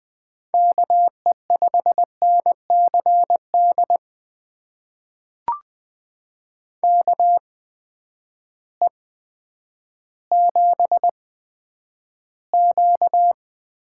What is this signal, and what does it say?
Morse code, heard as a keyed tone, about 20 wpm, spaced as for 10 wpm, 700 Hz. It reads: KE5NCD KE7Q
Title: USA Call Signs - 6 Characters - 8x Character Spacing - 20wpm